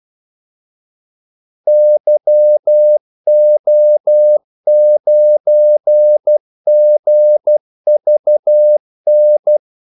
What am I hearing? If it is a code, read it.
Morse code, heard as a keyed tone, about 12 wpm, 600 Hz. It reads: YO9GVN
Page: 2 ekstra ordmellemrum mellem kaldesignalerne